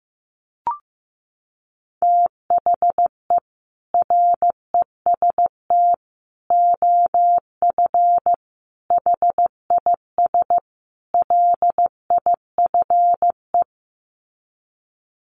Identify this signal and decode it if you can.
Morse code, heard as a keyed tone, about 15 wpm, 700 Hz. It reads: THE REST OF HIS LIFE
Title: Sets of 5 Words 15wpm